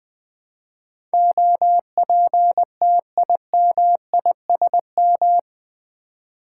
Morse code, heard as a keyed tone, about 21 wpm, 700 Hz. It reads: OPTIMISM
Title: Build A Word - 3 Letters At A Time - 8-Letter Words - 20wpm